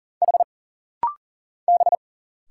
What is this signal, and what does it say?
Morse code, heard as a keyed tone, about 40 wpm, 700 Hz. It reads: H B
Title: Single Letter-Number Alt - Rapid-Fire 40wpm